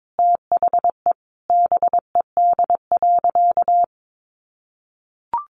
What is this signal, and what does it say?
Morse code, heard as a keyed tone, about 22 wpm, 700 Hz. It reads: THE BED.
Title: Sentences from Top 1400 Words 22wpm